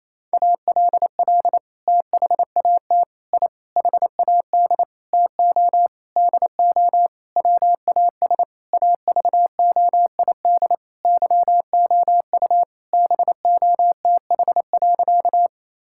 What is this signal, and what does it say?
Morse code, heard as a keyed tone, about 28 wpm, 690 Hz. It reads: ALL THAT I HAD TO DO WAS AVOID YOU BOTH.
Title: Sentences from Top 1000 Words - Review Part 6 - 28wpm